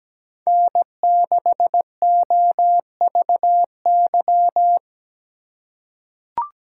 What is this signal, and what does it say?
Morse code, heard as a keyed tone, about 17 wpm, 700 Hz. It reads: N6OVY
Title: USA Call Signs - 5 Characters - 6x Character Spacing - 17wpm